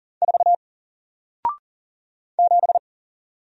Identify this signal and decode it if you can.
Morse code, heard as a keyed tone, about 40 wpm, 700 Hz. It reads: 4 7
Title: Single Letter-Number Alt 40wpm